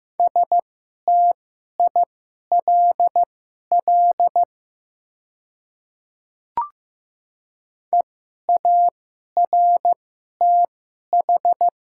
Morse code, heard as a keyed tone, about 15 wpm, spaced as for 11 wpm, 700 Hz. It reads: STILL EARTH
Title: 2x Character Spacing - Top 500 Words - 15wpm